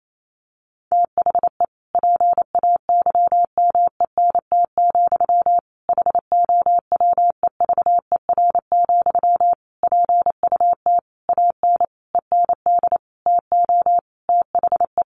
Morse code, heard as a keyed tone, about 28 wpm, 700 Hz. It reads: THE PAYMENT, HOWEVER, PUT AN END TO THE
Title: Sentences from Top 1200 Words 28wpm